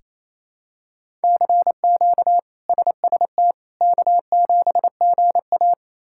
Morse code, heard as a keyed tone, about 28 wpm, 700 Hz. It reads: CQ SST K7GA
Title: K1USN SST QSO Practice - 28wpm